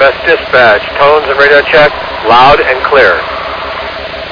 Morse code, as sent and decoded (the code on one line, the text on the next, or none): none